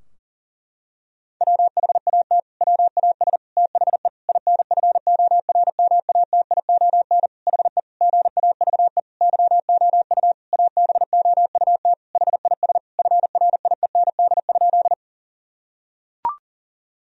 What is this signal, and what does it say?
Morse code, heard as a keyed tone, about 37 wpm, 700 Hz. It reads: WHAT WAS THE INFORMATION HE GAVE YOU ABOUT HIS FRIEND?
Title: Sentences from Top 1000 Words - Review Part 3 - 40wpm